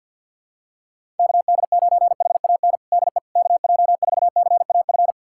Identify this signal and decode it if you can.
Morse code, heard as a keyed tone, about 51 wpm, 700 Hz. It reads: KD9HAN DE KJ4CAF
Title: USA Callsign DE Callsign 50wpm